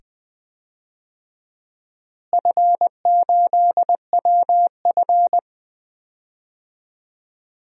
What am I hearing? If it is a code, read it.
Morse code, heard as a keyed tone, about 20 wpm, 700 Hz. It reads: F8WF
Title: DXCC-100 Call Signs - 4 Characters - Repeated 2x - 20wpm